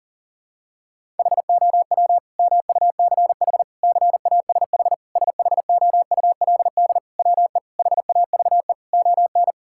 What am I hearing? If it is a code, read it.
Morse code, heard as a keyed tone, about 40 wpm, 700 Hz. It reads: HOW MUCH CASH SHOULD WE HAVE ON
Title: Sentences from Top 1000 Words - Review Part 5 - 40wpm